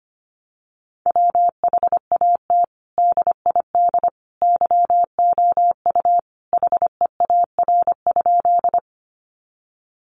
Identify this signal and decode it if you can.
Morse code, heard as a keyed tone, about 25 wpm, 700 Hz. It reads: WHAT DID YOU HEAR?
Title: Sentences from Top 300 Words 25wpm